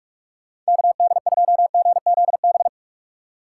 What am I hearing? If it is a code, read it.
Morse code, heard as a keyed tone, about 42 wpm, 700 Hz. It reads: KD2GZB
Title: USA Call Signs - 6 Characters - 3x Character Spacing - 45wpm